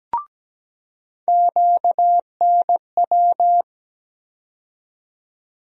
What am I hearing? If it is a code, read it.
Morse code, heard as a keyed tone, about 17 wpm, 700 Hz. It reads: QNW
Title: USA Call Sign Postfixes - 3 Letters 17wpm